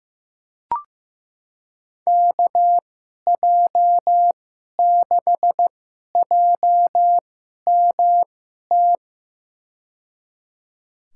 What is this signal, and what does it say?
Morse code, heard as a keyed tone, about 16 wpm, spaced as for 11 wpm, 700 Hz. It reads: KJ6JMT
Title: USA Call Signs - 6 Characters - 2x Character Spacing - 15wpm